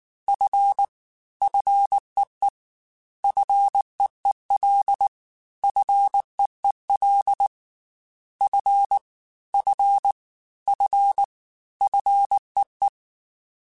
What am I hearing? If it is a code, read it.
Morse code, heard as a keyed tone, about 19 wpm, 800 Hz. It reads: F FEE FEEL FEEL F F F FEE
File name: Fs / Single F mp3s / Single F Prefixes 3x2 Twice @18